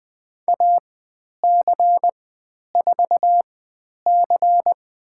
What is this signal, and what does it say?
Morse code, heard as a keyed tone, about 20 wpm, spaced as for 10 wpm, 700 Hz. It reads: AC4C
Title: 095 Lesson 16g - Letter c - Call Signs 20wpm